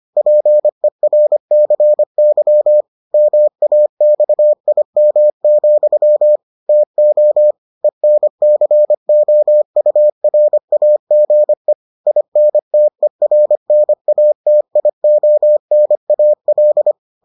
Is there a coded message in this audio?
Yes: Morse code from 0.2 to 16.9 s, heard as a keyed tone, about 25 wpm, 600 Hz. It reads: PERCY MAXIM, TO ENCOURAGE INTERNATIONAL